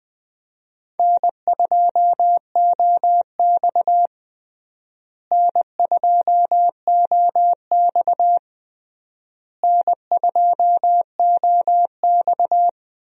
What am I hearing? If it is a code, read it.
Morse code, heard as a keyed tone, about 21 wpm, 700 Hz. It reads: N2OX N2OX N2OX
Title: USA Call Signs - 4 Characters - Repeated 3x - 20wpm